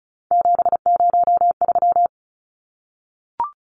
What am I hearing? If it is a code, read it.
Morse code, heard as a keyed tone, about 34 wpm, 700 Hz. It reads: 703
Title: Numbers - 3-Digit 35wpm